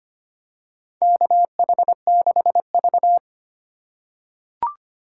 Morse code, heard as a keyed tone, about 25 wpm, 700 Hz. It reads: KH6V
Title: USA Call Signs - 4 Characters - 3x Character Spacing - 25wpm